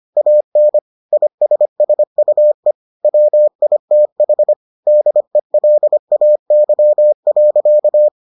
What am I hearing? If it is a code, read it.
Morse code, heard as a keyed tone, about 25 wpm, 600 Hz. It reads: AN ISSUE WITH DELAY.